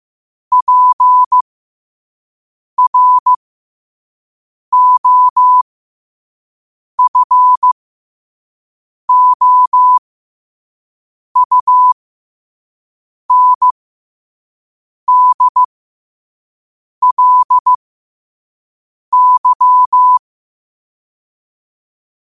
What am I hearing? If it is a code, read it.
Morse code, heard as a keyed tone, about 15 wpm, spaced as for 5 wpm, 1000 Hz. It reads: PROFOUNDLY
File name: qotdfiles / 5wpm